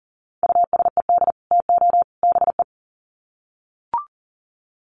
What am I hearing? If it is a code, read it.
Morse code, heard as a keyed tone, about 40 wpm, 700 Hz. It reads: USED TO BE
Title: Sets of 3 Words 40wpm